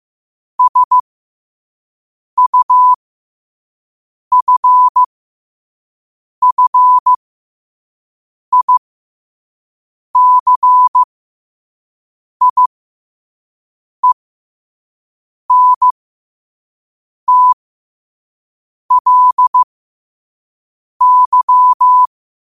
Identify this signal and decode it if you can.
Morse code, heard as a keyed tone, about 15 wpm, spaced as for 5 wpm, 1000 Hz. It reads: SUFFICIENTLY
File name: qotdfiles / 5wpm